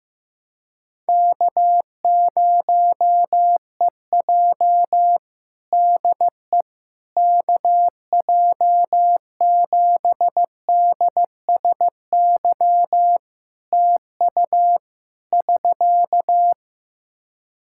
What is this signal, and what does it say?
Morse code, heard as a keyed tone, about 15 wpm, 700 Hz. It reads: K0EJ DE KJ7DSY TU <SK>
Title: Practice QSO - Exchange 4 - Ending - 15wpm